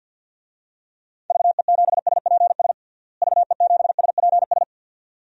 Morse code, heard as a keyed tone, about 48 wpm, 700 Hz. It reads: VE7SPS VE7SPS
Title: DXCC-100 Call Signs - 6 Characters - Repeated 2x - 50wpm